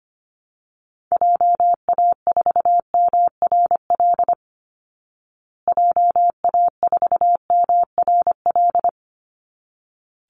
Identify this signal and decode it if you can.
Morse code, heard as a keyed tone, about 25 wpm, 700 Hz. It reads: JA4MRL JA4MRL
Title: Call Signs Repeated 3x - 25wpm